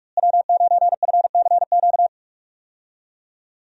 Morse code, heard as a keyed tone, about 46 wpm, 700 Hz. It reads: W9FCQ
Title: USA Call Signs - 5 Characters - Repeated 3x - 45wpm